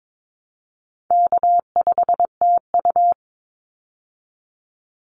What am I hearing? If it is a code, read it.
Morse code, heard as a keyed tone, about 22 wpm, 700 Hz. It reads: K5TU
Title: Call Signs Repeated 3x - 22wpm